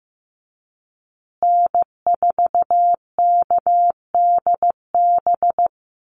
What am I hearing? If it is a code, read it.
Morse code, heard as a keyed tone, about 15 wpm, 700 Hz. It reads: N4KDB
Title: USA Call Signs Repeated 2x 15wpm